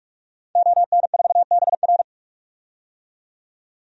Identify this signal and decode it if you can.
Morse code, heard as a keyed tone, about 46 wpm, 700 Hz. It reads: ON4BR
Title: DXCC-100 Call Signs - 5 Characters - 45wpm